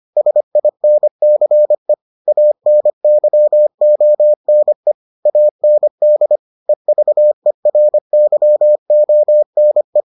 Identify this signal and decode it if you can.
Morse code, heard as a keyed tone, about 25 wpm, 600 Hz. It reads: SINCE ANYONE AND EVERYONE